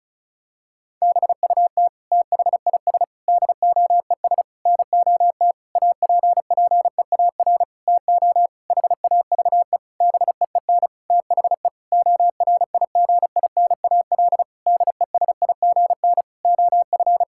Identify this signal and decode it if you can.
Morse code, heard as a keyed tone, about 36 wpm, 700 Hz. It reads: BUT THIS DOES NOT APPEAR TO HAVE BEEN THE ORIGINAL DESIGN OF
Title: Sentences from Top 1000 Words - Review Part 6 - 35wpm